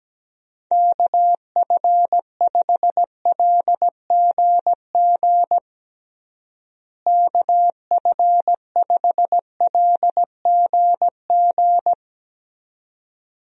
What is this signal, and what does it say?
Morse code, heard as a keyed tone, about 17 wpm, 700 Hz. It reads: KF5LGG KF5LGG
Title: USA Call Signs - 6 Characters - Repeated 2x - 17wpm